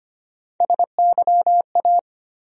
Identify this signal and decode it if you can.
Morse code, heard as a keyed tone, about 25 wpm, 700 Hz. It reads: SYA
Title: USA Call Sign Postfixes - 3 Letters 25wpm